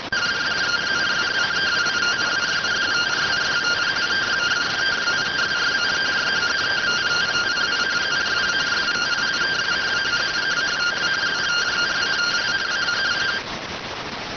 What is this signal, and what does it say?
Two-tone teleprinter signal, about 170 Hz shift.